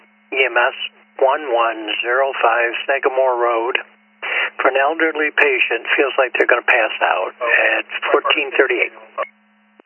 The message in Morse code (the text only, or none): none